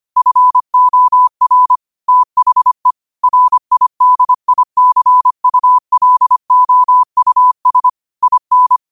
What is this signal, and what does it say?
Morse code, heard as a keyed tone, about 25 wpm, 1000 Hz. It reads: FOR THE RIDICULOUS IN
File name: qotdfiles / 25wpm